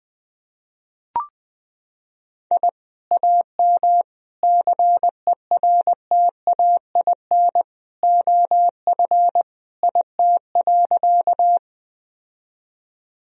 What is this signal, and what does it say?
Morse code, heard as a keyed tone, about 20 wpm, 700 Hz. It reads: I AM CERTAIN OF IT.
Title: Sentences from Top 1000 Words - Review Part 2 - 20wpm